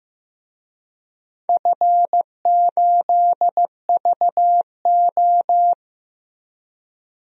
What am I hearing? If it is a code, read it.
Morse code, heard as a keyed tone, about 15 wpm, 700 Hz. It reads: F8VO
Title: DXCC-100 Call Signs - 4 Characters - Repeated 3x - 15wpm